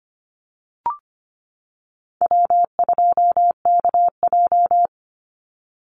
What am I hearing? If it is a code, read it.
Morse code, heard as a keyed tone, about 25 wpm, 700 Hz. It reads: W2KJ
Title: Call Signs - 25wpm